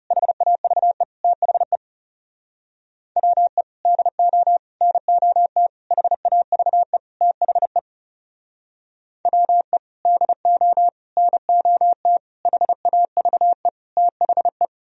Morse code, heard as a keyed tone, about 35 wpm, 700 Hz. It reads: HAVE THE WE DO NOT HAVE THE WE DO NOT HAVE THE
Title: Speed-Racing - Sets of 5 Words 30wpm